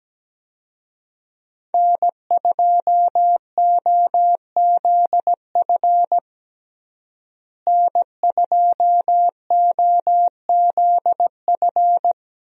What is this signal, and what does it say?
Morse code, heard as a keyed tone, about 17 wpm, 700 Hz. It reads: N2OZF N2OZF
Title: USA Call Signs - 5 Characters - Repeated 2x - 17wpm